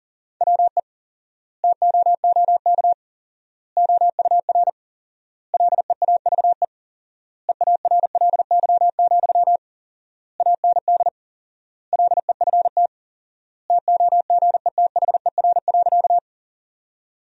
Morse code, heard as a keyed tone, about 40 wpm, 700 Hz. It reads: WE TOOK OUR LEAVE EARLY, AND LEFT TOGETHER.
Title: 4x Word-Spacing - Sentences from Top 400 Words - 40wpm